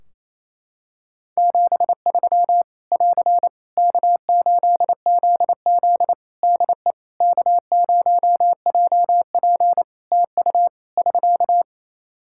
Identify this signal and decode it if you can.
Morse code, heard as a keyed tone, about 28 wpm, 700 Hz. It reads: 73 <AR> K8ZZ DE K0JP TU <SK>
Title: Practice QSO - Exchange 4 - Ending - 28wpm